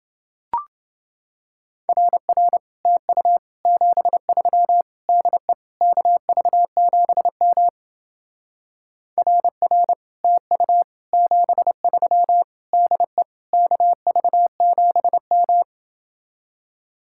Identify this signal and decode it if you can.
Morse code, heard as a keyed tone, about 30 wpm, 700 Hz. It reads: RR TU 73 DE KV7M RR TU 73 DE KV7M